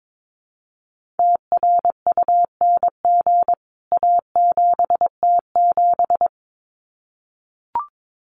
Morse code, heard as a keyed tone, about 22 wpm, 700 Hz. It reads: TRUNG A7T7
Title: CWT Exchange 22wpm